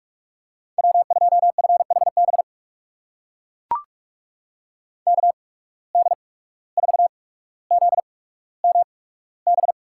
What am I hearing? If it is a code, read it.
Morse code, heard as a keyed tone, about 43 wpm, 690 Hz. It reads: W 2 F H B K D 4 Z M B
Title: USA Call Signs - 8x Character Spacing - 45wpm